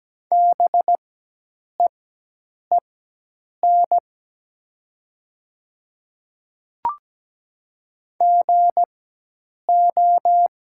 Morse code, heard as a keyed tone, about 17 wpm, spaced as for 8 wpm, 700 Hz. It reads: BEEN GO
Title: 4x Character Spacing - Top 200 Words - 17wpm